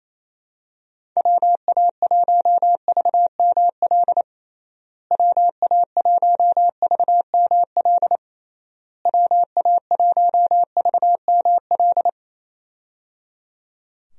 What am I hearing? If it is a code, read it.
Morse code, heard as a keyed tone, about 28 wpm, 700 Hz. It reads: WA1VML WA1VML WA1VML
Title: USA Call Signs - 6 Characters - Repeated 3x - 28wpm